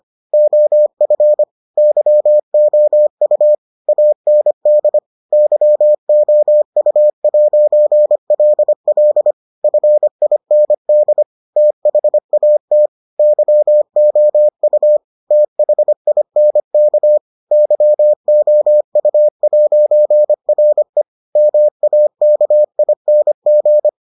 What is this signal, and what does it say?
Morse code, heard as a keyed tone, about 25 wpm, 600 Hz. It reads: OF YOU AND YOU'LL FIND THAT YOU THINK YOU'RE MAKING